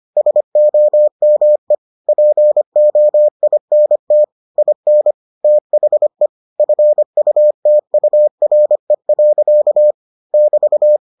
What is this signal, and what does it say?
Morse code, heard as a keyed tone, about 25 wpm, 600 Hz. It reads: SOME POINT IN THE FUTURE. <BT>